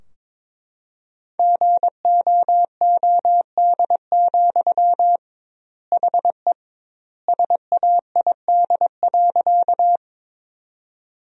Morse code, heard as a keyed tone, about 22 wpm, 700 Hz. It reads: GOOD, HE SAID.
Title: 2x Word-Spacing - Sentences from Top 200 Words - 22wpm